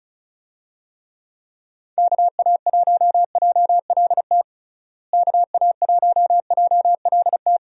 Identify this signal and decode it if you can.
Morse code, heard as a keyed tone, about 34 wpm, 700 Hz. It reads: KA1JLT KA1JLT
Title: USA Call Signs - 6 Characters - Repeated 2x - 35wpm